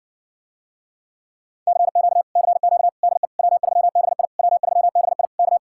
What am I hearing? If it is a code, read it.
Morse code, heard as a keyed tone, about 60 wpm, 700 Hz. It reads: CQ CQ DE W3BI W3BI K